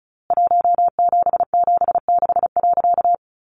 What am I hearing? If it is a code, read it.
Morse code, heard as a keyed tone, about 34 wpm, 700 Hz. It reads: 1776.
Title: A Sound Of Thunder 35wpm